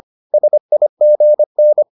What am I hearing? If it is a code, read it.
Morse code, heard as a keyed tone, about 25 wpm, 600 Hz. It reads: SIGN